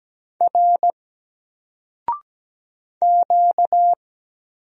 Morse code, heard as a keyed tone, about 17 wpm, 700 Hz. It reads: R Q
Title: Single Letter-Number Alt 17wpm